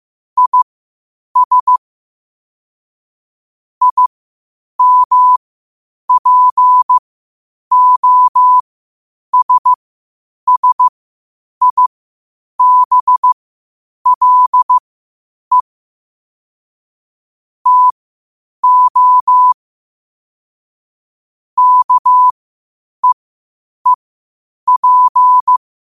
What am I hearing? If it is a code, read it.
Morse code, heard as a keyed tone, about 15 wpm, spaced as for 8 wpm, 1000 Hz. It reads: IS IMPOSSIBLE TO KEEP
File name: qotdfiles / 8wpm